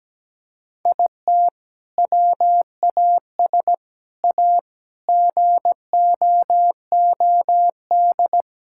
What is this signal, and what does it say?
Morse code, heard as a keyed tone, about 17 wpm, 700 Hz. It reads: IT WAS A GOOD
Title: Sentences from Top 1000 Words - Review Part 5 - 17wpm